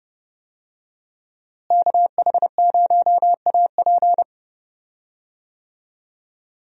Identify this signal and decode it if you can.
Morse code, heard as a keyed tone, about 29 wpm, 700 Hz. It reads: KH0AP
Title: USA Call Signs - 5 Characters - Repeated 3x - 30wpm